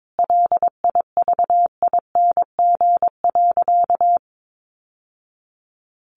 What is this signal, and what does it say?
Morse code, heard as a keyed tone, about 22 wpm, 700 Hz. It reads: LIVING.
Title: Sentences from Top 1100 Words 22wpm